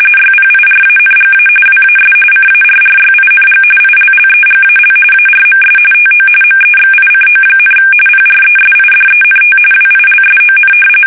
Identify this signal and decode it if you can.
RTTY, about 45.45 baud. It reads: YPIRANGA WAS CARRYING ILLEGAL ARMS TO MEXICO, PRESIDENT WILSON
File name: fsk850